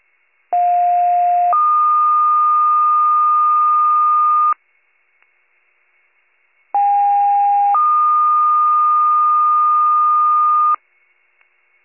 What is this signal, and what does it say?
Two-tone page, heard as a one-second tone, then a three-second tone, about 710 then 1220 Hz, about 810 then 1220 Hz. It is heard from 0.5 to 10.8 s.